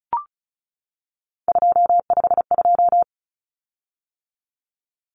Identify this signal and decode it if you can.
Morse code, heard as a keyed tone, about 34 wpm, 700 Hz. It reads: 252
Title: Common Cut Numbers - 3-Digit 35wpm